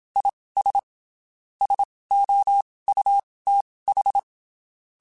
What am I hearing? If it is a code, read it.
Morse code, heard as a keyed tone, about 27 wpm, spaced as for 19 wpm, 800 Hz. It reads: IS SOUTH
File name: (spaced) / QSOs Spaced-mp3s / kc7haa @25 sp